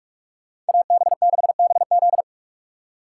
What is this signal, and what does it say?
Morse code, heard as a keyed tone, about 54 wpm, 700 Hz. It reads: AB6BZ
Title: Speed Racing - USA 2x2 Call Signs 45wpm